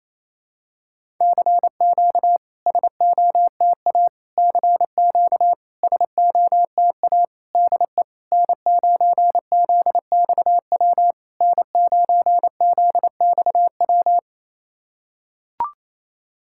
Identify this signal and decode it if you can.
Morse code, heard as a keyed tone, about 28 wpm, 690 Hz. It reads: CQ SOTA CQ SOTA DE N9ZXW N9ZXW
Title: SOTA QSO Practice 28wpm